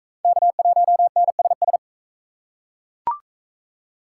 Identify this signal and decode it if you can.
Morse code, heard as a keyed tone, about 36 wpm, 700 Hz. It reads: K1NSS